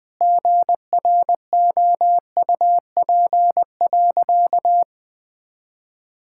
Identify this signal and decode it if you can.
Morse code, heard as a keyed tone, about 20 wpm, 700 Hz. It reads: GROUP.
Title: Sentences with 5-Word Phrases - 20wpm